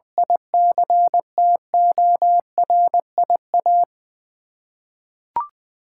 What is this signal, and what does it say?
Morse code, heard as a keyed tone, about 20 wpm, 700 Hz. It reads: ICTORIA
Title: Practice QSO - Exchange 1 - Beginning - 20wpm